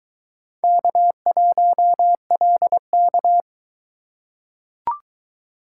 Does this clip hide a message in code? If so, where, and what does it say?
Yes: Morse code from 0.6 to 3.4 s, heard as a keyed tone, about 22 wpm, 700 Hz. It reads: K1LK